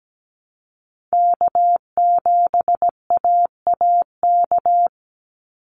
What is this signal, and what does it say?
Morse code, heard as a keyed tone, about 17 wpm, 700 Hz. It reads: K7AAK
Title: USA Call Signs 17wpm